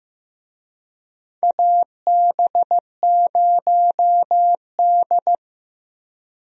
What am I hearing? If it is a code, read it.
Morse code, heard as a keyed tone, about 15 wpm, 700 Hz. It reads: AB0D
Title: USA Call Signs - 4 Characters - Repeated 2x - 15wpm